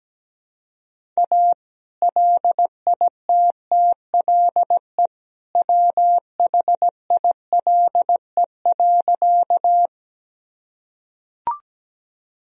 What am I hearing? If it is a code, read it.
Morse code, heard as a keyed tone, about 17 wpm, 700 Hz. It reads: A LITTLE WHILE.
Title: Sentences from Top 1000 Words - Review Part 1 - 17wpm